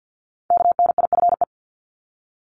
Morse code, heard as a keyed tone, about 50 wpm, 700 Hz. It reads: KNIFE
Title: Multi-letter Phonemes 50wpm